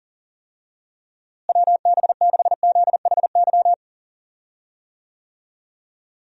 Morse code, heard as a keyed tone, about 40 wpm, 700 Hz. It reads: WB6ZHY
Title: USA Call Signs - 6 Characters - Repeated 3x - 40wpm